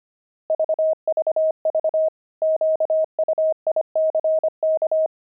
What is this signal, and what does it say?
Morse code, heard as a keyed tone, about 25 wpm, 630 Hz. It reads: VVV QUICK
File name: pangram-fs-25-cs-25-zephyrs